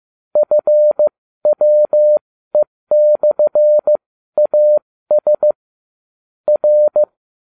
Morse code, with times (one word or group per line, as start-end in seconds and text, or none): 0.4-5.5 FWE/AS
6.5-7.0 R